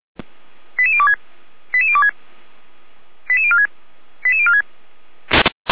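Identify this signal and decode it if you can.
Five-tone selective call (ZVEI), heard as quick runs of five tones, tones of about 70 ms. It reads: 88026 88026 88046 88046
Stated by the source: ZVEI five-tone call received by SDR, feuerwehr-orlamuende.org fire brigade.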